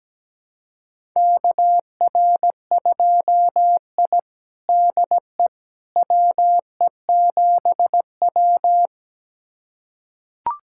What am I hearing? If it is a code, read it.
Morse code, heard as a keyed tone, about 17 wpm, 700 Hz. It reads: KR2I DE WE7W
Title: USA Callsign DE Callsign - 4 Characters - 17wpm